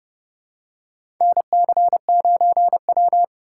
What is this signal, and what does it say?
Morse code, heard as a keyed tone, about 30 wpm, 700 Hz. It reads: NC9W
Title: K1USN SST QSO Practice - 30wpm